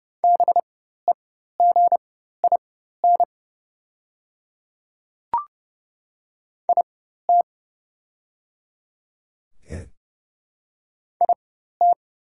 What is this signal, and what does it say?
Morse code, heard as a keyed tone, about 30 wpm, spaced as for 14 wpm, 700 Hz. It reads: BEGIN IT IT
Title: 4x Character Spacing - Top 500 Words - 30wpm